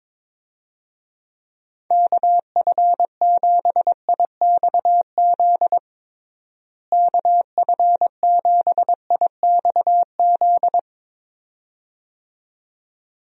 Morse code, heard as a keyed tone, about 22 wpm, 700 Hz. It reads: KF7IXZ KF7IXZ
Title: USA Call Signs - 6 Characters - Repeated 2x - 22wpm